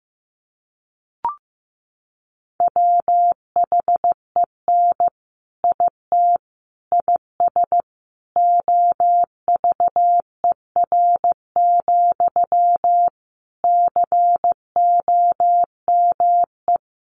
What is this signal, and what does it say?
Morse code, heard as a keyed tone, about 15 wpm, 700 Hz. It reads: WHEN IT IS OVER, COME
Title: Sentences from Top 1200 Words 15wpm